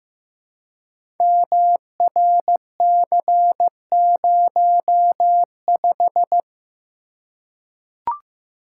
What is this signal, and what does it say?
Morse code, heard as a keyed tone, about 15 wpm, 700 Hz. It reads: MRC05
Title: Code Groups - 5 Letter:Numbers - 15wpm